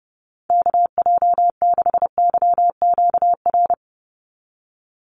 Morse code, heard as a keyed tone, about 30 wpm, 700 Hz. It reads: KJ6YQR
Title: USA Call Signs 30wpm